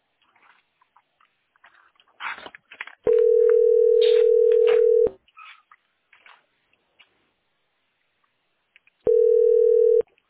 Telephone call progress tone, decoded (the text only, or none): ringback tone